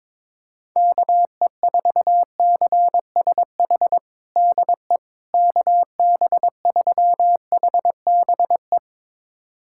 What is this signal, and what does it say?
Morse code, heard as a keyed tone, about 22 wpm, 700 Hz. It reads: KE4CSH DE KB3HBE
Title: USA Callsign DE Callsign 22wpm